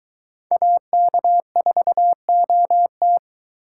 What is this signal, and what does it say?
Morse code, heard as a keyed tone, about 22 wpm, 700 Hz. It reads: AK4OT